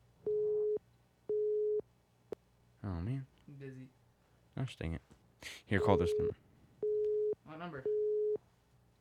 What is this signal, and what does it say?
Busy tone (Europe, 425 Hz), with speech.